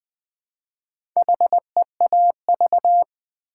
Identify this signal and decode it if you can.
Morse code, heard as a keyed tone, about 20 wpm, 700 Hz. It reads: HEAV
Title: Build A Word - 2 Letters At A Time - 6-Letter Words - 20wpm